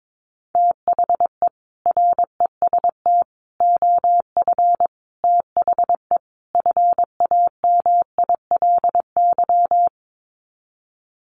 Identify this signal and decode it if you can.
Morse code, heard as a keyed tone, about 22 wpm, 700 Hz. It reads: THE REST OF THE FAMILY
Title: Sets of 5 Words Encore2 22wpm